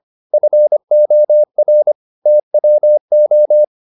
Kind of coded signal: Morse code